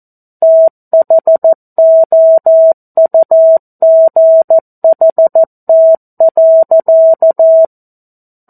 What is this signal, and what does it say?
Morse code, heard as a keyed tone, about 14 wpm, 650 Hz. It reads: THOUGHT.
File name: 02-Chapter-02 14 WPM